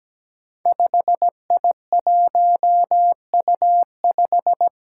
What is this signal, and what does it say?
Morse code, heard as a keyed tone, about 17 wpm, 700 Hz. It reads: 5I1U5
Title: Five Challenging Letter-Number Combinations 17wpm